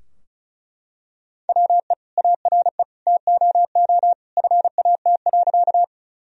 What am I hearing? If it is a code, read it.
Morse code, heard as a keyed tone, about 34 wpm, 700 Hz. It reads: WE ARE TOO FAT.
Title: Sentences from Top 1000 Words - Review Part 4 - 35wpm